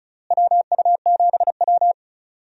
Morse code, heard as a keyed tone, about 34 wpm, 700 Hz. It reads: WU7W
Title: USA Call Signs - 4 Characters - Repeated 2x - 35wpm